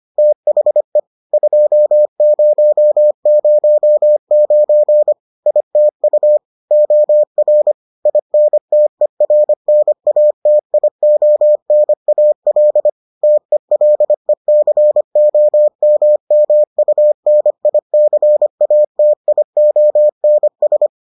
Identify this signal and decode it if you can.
Morse code, heard as a keyed tone, about 25 wpm, 600 Hz. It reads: THE 2009 ITU OR INTERNATIONAL TELECOMMUNICATIONS